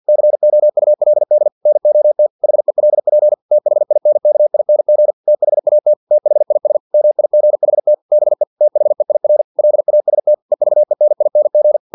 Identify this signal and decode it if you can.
Morse code, heard as a keyed tone, about 50 wpm, 600 Hz. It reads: COULD NOT HELP THINKING THAT THIS MIGHT BE THEIR LAST EVENING
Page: Never 49wpm